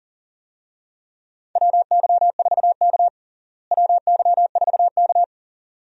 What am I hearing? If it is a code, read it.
Morse code, heard as a keyed tone, about 40 wpm, 700 Hz. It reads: WY4K WY4K
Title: USA Call Signs - 4 Characters - Repeated 2x - 40wpm